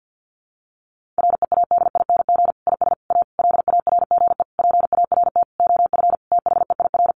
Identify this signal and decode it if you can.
Morse code, heard as a keyed tone, about 50 wpm, 700 Hz. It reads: READING IS A LARGE PART OF THEIR
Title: Sentences from Top 1400 Words 50wpm